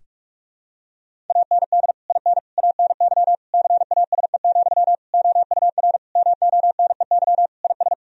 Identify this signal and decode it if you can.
Morse code, heard as a keyed tone, about 44 wpm, 700 Hz. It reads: AND IN ANY CASE, OUR MONEY IS
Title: Sentences from Top 1000 Words - Review Part 1 - 45wpm